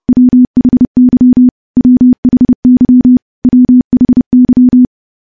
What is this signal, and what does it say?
Morse code, heard as a keyed tone, about 30 wpm, 250 Hz. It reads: WHY WHY WHY
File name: morse / morse-training-30WPM